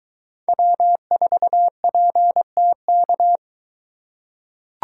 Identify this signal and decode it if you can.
Morse code, heard as a keyed tone, about 23 wpm, 700 Hz. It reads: W4PTK